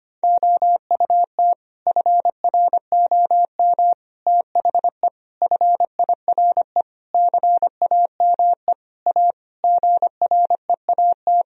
Morse code, heard as a keyed tone, about 25 wpm, 700 Hz. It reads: OUT FROM THE FIRE CAME A GREAT
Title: Sentences from Top 1000 Words - Review Part 3 - 25wpm